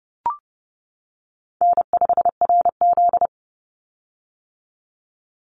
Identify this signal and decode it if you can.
Morse code, heard as a keyed tone, about 30 wpm, 700 Hz. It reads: N5RZ
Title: Easy Call Signs - 30wpm